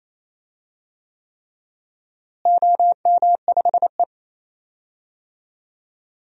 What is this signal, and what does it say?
Morse code, heard as a keyed tone, about 28 wpm, 700 Hz. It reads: OM5E
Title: DXCC-100 Call Signs - 4 Characters - Repeated 3x - 28wpm